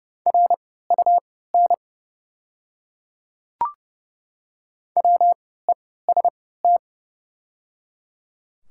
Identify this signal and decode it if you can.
Morse code, heard as a keyed tone, about 30 wpm, spaced as for 17 wpm, 700 Hz. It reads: RUN WEST
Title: 3x Character Spacing - Top 500 Words - 30wpm